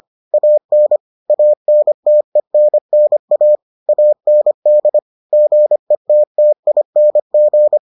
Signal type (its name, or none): Morse code